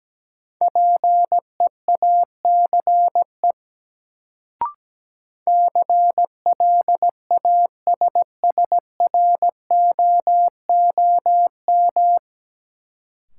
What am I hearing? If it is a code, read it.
Morse code, heard as a keyed tone, about 17 wpm, 700 Hz. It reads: PEACE CLASSROOM
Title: Sentences from Top 1500 Words - New Word Drill Alt - 17wpm